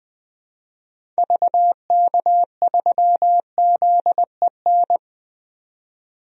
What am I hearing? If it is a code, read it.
Morse code, heard as a keyed tone, about 20 wpm, 700 Hz. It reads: VK3ZEN
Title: DXCC-100 Call Signs - 6 Characters - 20wpm